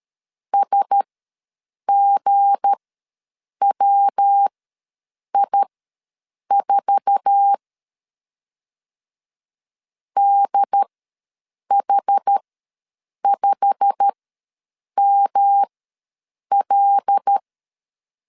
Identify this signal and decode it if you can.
Morse code, heard as a keyed tone, about 13 wpm, spaced as for 7 wpm, 800 Hz. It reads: SGWI4 DH5ML
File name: LEZIONI / 9 286 ws 12wpm 3351